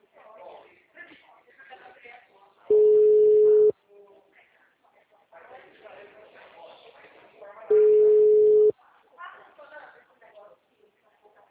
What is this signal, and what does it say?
Ringback tone (Europe, 425 Hz), with speech.